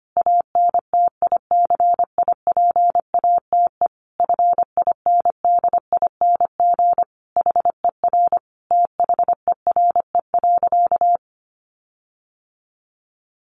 Morse code, heard as a keyed tone, about 25 wpm, 700 Hz. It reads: ANTICIPATE FINDING HER THERE.
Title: Sentences from Top 1500 Words 25wpm